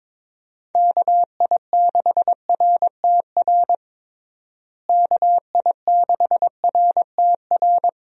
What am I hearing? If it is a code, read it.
Morse code, heard as a keyed tone, about 21 wpm, 700 Hz. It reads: KI6RTR KI6RTR
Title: USA Call Signs - 6 Characters - Repeated 2x - 22wpm